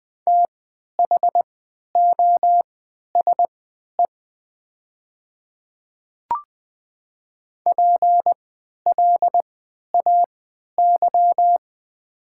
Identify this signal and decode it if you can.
Morse code, heard as a keyed tone, about 20 wpm, spaced as for 11 wpm, 700 Hz. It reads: THOSE PLAY
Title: 3x Character Spacing - Top 500 Words - 20wpm